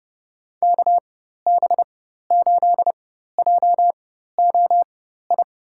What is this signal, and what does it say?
Morse code, heard as a keyed tone, about 30 wpm, spaced as for 14 wpm, 700 Hz. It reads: KB8JOI
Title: USA Call Signs - 6 Characters - 4x Character Spacing - 30wpm